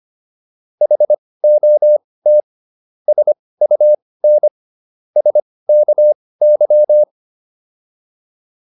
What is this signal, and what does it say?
Morse code, heard as a keyed tone, about 25 wpm, spaced as for 18 wpm, 600 Hz. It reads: HOT SUN SKY